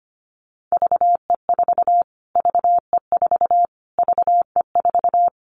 Morse code, heard as a keyed tone, about 25 wpm, 700 Hz. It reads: VE4 VE4 VE4
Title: Callsign Prefixes for Top 25 DXCC - Repeated 3x - 25wpm